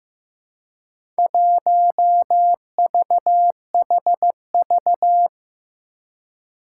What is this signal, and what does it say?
Morse code, heard as a keyed tone, about 15 wpm, 700 Hz. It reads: 1VHV
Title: Four Challenging Letter-Number Combinations 15wpm